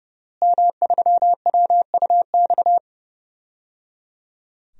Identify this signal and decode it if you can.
Morse code, heard as a keyed tone, about 30 wpm, 700 Hz. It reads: M3WUX
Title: DXCC-100 Call Signs - 5 Characters - Repeated 3x - 30wpm